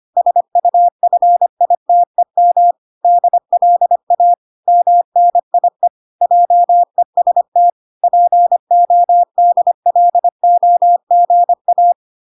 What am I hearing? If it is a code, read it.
Morse code, heard as a keyed tone, about 25 wpm, 700 Hz. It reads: SUFITEM DLA MNIE JEST PODLOGA